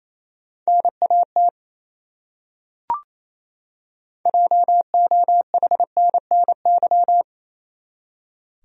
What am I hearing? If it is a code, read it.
Morse code, heard as a keyed tone, about 28 wpm, 700 Hz. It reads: NAT JOHNNY
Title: USA Common Names - 28wpm